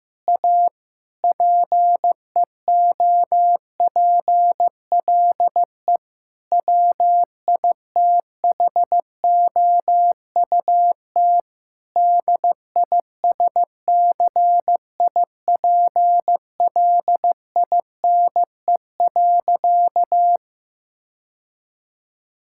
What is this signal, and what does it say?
Morse code, heard as a keyed tone, about 16 wpm, 700 Hz. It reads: A PEOPLE WITHOUT DISCIPLINE.
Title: Sentences from Top 1000 Words - Review Part 6 - 15wpm